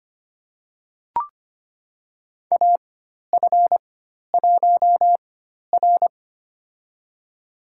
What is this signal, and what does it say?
Morse code, heard as a keyed tone, about 25 wpm, spaced as for 12 wpm, 690 Hz. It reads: AF1R
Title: USA Call Signs - 4x Character Spacing - 25wpm